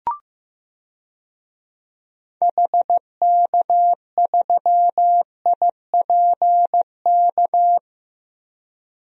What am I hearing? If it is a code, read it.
Morse code, heard as a keyed tone, about 15 wpm, 700 Hz. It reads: HK3IPK
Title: DXCC-100 Call Signs - 6 Characters - Repeated 3x - 15wpm